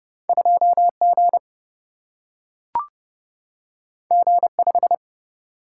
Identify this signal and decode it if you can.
Morse code, heard as a keyed tone, about 30 wpm, 700 Hz. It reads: 2G G5
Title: Single-Letter Single-Number 30wpm